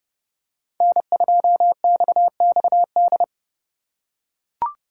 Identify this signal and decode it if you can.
Morse code, heard as a keyed tone, about 29 wpm, 700 Hz. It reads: N2XXD